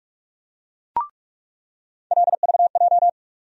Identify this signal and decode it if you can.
Morse code, heard as a keyed tone, about 42 wpm, 700 Hz. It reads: LVJ
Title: USA Call Sign Postfixes - 3 Letters 45wpm